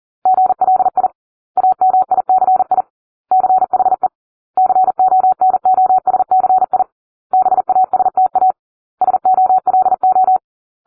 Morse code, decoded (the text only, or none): GLI AMICI CHE CONOSCI BASTA SOLO